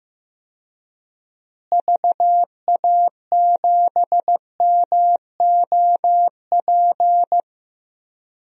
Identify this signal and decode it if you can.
Morse code, heard as a keyed tone, about 15 wpm, 700 Hz. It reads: VA7MOP
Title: DXCC-100 Call Signs - 6 Characters - Repeated 3x - 15wpm